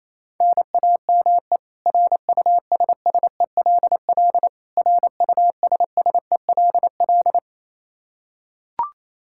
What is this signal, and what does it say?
Morse code, heard as a keyed tone, about 27 wpm, 700 Hz. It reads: NAME RUSSELL RUSSELL
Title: K1USN SST QSO Practice 28wpm